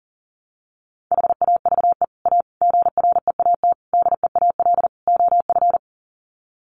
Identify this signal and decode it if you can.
Morse code, heard as a keyed tone, about 40 wpm, 700 Hz. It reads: HAVE A GREAT DEAL OF
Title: Sets of 5 Words Encore2 40wpm